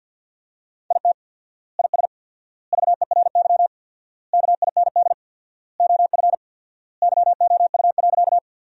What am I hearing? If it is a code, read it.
Morse code, heard as a keyed tone, about 50 wpm, 700 Hz. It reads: IT IS VERY KIND OF YOU.
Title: 4x Word-Spacing - Sentences from Top 200 Words - 50wpm